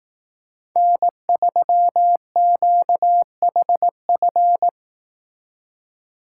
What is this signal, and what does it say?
Morse code, heard as a keyed tone, about 18 wpm, 700 Hz. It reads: N3QHF
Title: Speed Racing - USA 1x3 Call Signs 15wpm